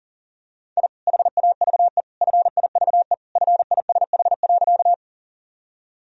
Morse code, heard as a keyed tone, about 37 wpm, 700 Hz. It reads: I HAVE FIVE FISH.
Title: Sentences from Top 1000 Words - Review Part 2 - 40wpm